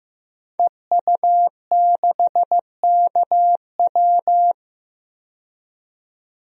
Morse code, heard as a keyed tone, about 15 wpm, 700 Hz. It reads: EU6KW
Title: DXCC-100 Call Signs - 5 Characters - 15wpm